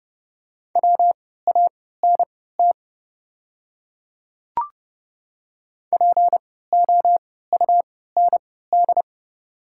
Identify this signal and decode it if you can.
Morse code, heard as a keyed tone, about 30 wpm, spaced as for 17 wpm, 690 Hz. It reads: WANT POUND
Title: 3x Character Spacing - Top 500 Words - 30wpm